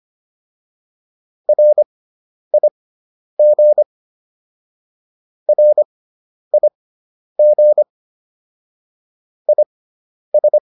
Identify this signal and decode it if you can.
Morse code, heard as a keyed tone, about 25 wpm, spaced as for 10 wpm, 600 Hz. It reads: RIG RIG IS